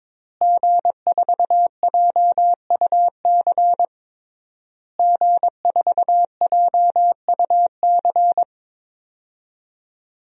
Morse code, heard as a keyed tone, about 22 wpm, 700 Hz. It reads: G4JUC G4JUC
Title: DXCC-100 Call Signs - 5 Characters - Repeated 3x - 22wpm